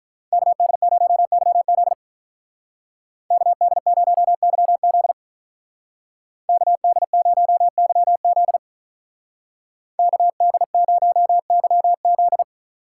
Morse code, heard as a keyed tone, about 47 wpm, 700 Hz. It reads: KD0YZ KD0YZ KD0YZ KD0YZ